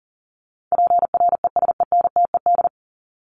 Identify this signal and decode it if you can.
Morse code, heard as a keyed tone, about 40 wpm, 700 Hz. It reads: PRESENTED
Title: ED Suffix - Top 100 Words with Base Words 40wpm